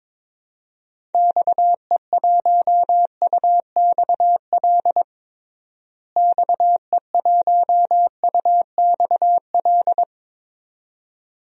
Morse code, heard as a keyed tone, about 22 wpm, 700 Hz. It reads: XE1UXL XE1UXL
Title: DXCC-100 Call Signs - 6 Characters - Repeated 2x - 22wpm